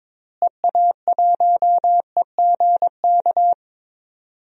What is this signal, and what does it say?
Morse code, heard as a keyed tone, about 22 wpm, 700 Hz. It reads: EA1EGK
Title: DXCC-100 Call Signs - 6 Characters - Repeated 2x - 22wpm